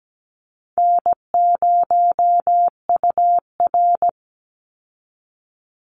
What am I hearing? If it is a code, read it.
Morse code, heard as a keyed tone, about 17 wpm, 700 Hz. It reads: N0UR
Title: Easy Call Signs - 17wpm